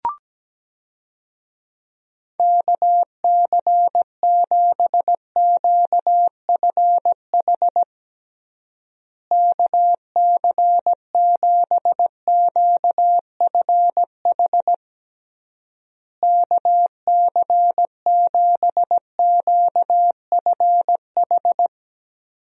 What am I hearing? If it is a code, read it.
Morse code, heard as a keyed tone, about 17 wpm, 700 Hz. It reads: KC7QFH KC7QFH KC7QFH
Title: USA Call Signs - 6 Characters - Repeated 3x - 17wpm